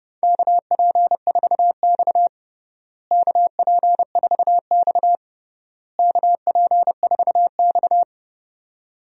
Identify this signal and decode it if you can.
Morse code, heard as a keyed tone, about 31 wpm, 700 Hz. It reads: KP4X KP4X KP4X
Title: DXCC-100 Call Signs - 4 Characters - Repeated 3x - 30wpm